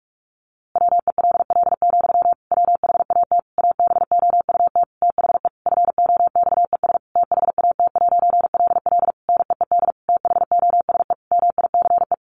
Morse code, heard as a keyed tone, about 44 wpm, 700 Hz. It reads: WELL, WHAT ABOUT THE FOXES THAT'LL NEED THOSE MICE
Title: A Sound Of Thunder 45wpm